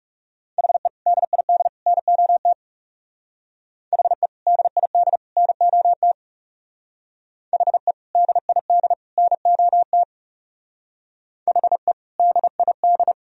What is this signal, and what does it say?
Morse code, heard as a keyed tone, about 40 wpm, 700 Hz. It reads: HE DID NOT HE DID NOT HE DID NOT HE DID